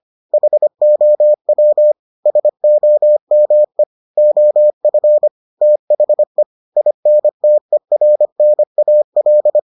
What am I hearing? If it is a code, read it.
Morse code, heard as a keyed tone, about 25 wpm, 600 Hz. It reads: HOW SOME OF THE INTERNAL